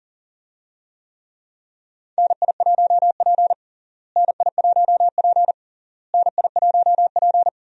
Morse code, heard as a keyed tone, about 40 wpm, 700 Hz. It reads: NI1P NI1P NI1P
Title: USA Call Signs - 4 Characters - Repeated 3x - 40wpm